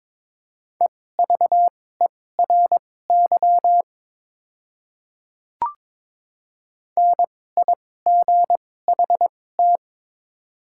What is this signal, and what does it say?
Morse code, heard as a keyed tone, about 22 wpm, spaced as for 16 wpm, 700 Hz. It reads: EVERY NIGHT
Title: 2x Character Spacing - Top 300 Words - 22wpm